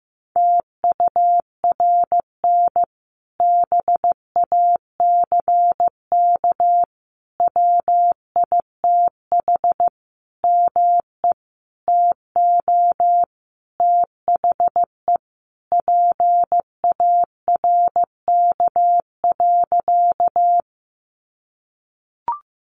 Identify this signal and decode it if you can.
Morse code, heard as a keyed tone, about 15 wpm, 700 Hz. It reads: TURN BACK WITH ME TO THE PARK.
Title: Sentences from Top 1300 Words 15wpm